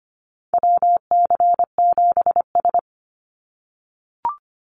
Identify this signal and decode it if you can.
Morse code, heard as a keyed tone, about 25 wpm, 700 Hz. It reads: WC7S
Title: Call Signs - 25wpm